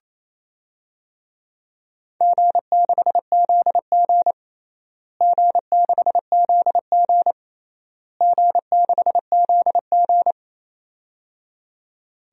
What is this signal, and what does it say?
Morse code, heard as a keyed tone, about 27 wpm, 700 Hz. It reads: G6ZG G6ZG G6ZG
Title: DXCC-100 Call Signs - 4 Characters - Repeated 3x - 28wpm